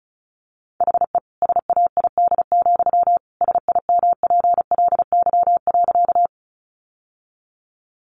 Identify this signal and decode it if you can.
Morse code, heard as a keyed tone, about 34 wpm, 700 Hz. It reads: HE SAID, SIMPLY.
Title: Sentences from Top 600 Words 35wpm